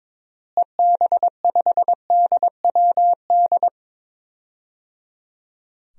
Morse code, heard as a keyed tone, about 21 wpm, 700 Hz. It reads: EB5DWD
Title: DXCC-100 Call Signs - 6 Characters - Repeated 3x - 22wpm